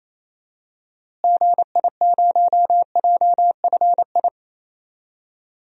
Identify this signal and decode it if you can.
Morse code, heard as a keyed tone, about 28 wpm, 700 Hz. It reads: GI0JFI
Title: DXCC-100 Call Signs - 6 Characters - Repeated 3x - 28wpm